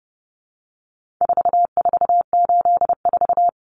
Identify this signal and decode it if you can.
Morse code, heard as a keyed tone, about 30 wpm, 700 Hz. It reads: 4484
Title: Common Cut Numbers - 4-Digit 30wpm